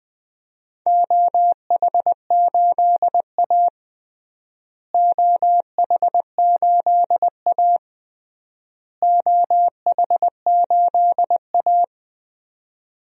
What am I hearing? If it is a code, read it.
Morse code, heard as a keyed tone, about 20 wpm, 700 Hz. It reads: OH8A OH8A OH8A
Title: DXCC-100 Call Signs - 4 Characters - Repeated 3x - 20wpm